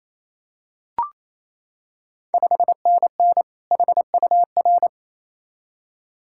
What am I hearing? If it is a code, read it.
Morse code, heard as a keyed tone, about 28 wpm, 690 Hz. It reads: 5NN HUR
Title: Ontario QSO Party Exchange 28wpm